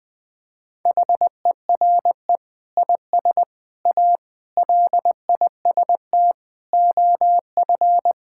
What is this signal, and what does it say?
Morse code, heard as a keyed tone, about 20 wpm, 700 Hz. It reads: HERE IS A LIST OF
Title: Sentences with 8-Word Phrases - 20wpm